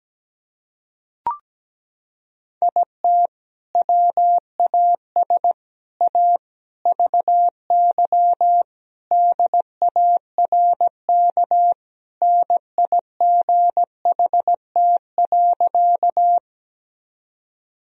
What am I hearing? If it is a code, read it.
Morse code, heard as a keyed tone, about 17 wpm, 700 Hz. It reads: IT WAS A VY DARK NIGHT.
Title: Sentences from Top 500 Words and Abbreviations 17wpm